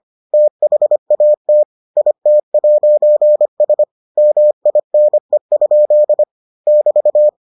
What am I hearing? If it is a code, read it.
Morse code, heard as a keyed tone, about 25 wpm, 600 Hz. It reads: THAT IT'S MINE? <BT>